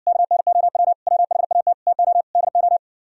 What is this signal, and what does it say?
Morse code, heard as a keyed tone, about 63 wpm, 700 Hz. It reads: KNOW WHAT TO DO